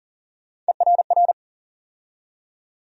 Morse code, heard as a keyed tone, about 40 wpm, 700 Hz. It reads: ERR
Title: USA Call Sign Postfixes - 3 Letters 40wpm